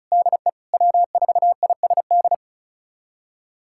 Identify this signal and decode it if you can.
Morse code, heard as a keyed tone, about 36 wpm, 700 Hz. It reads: DE W4ISD
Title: USA Callsign DE Callsign - 5 Characters - 35wpm